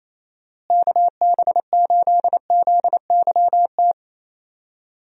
Morse code, heard as a keyed tone, about 28 wpm, 700 Hz. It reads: KB8ZYT
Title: USA Call Signs - 6 Characters - Repeated 3x - 28wpm